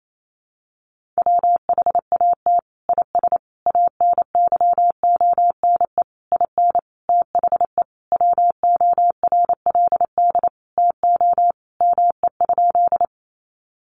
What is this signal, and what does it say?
Morse code, heard as a keyed tone, about 28 wpm, 700 Hz. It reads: WHAT IS ANYONE IN THE WORLD TO ME?
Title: Sentences from Top 900 Words 28wpm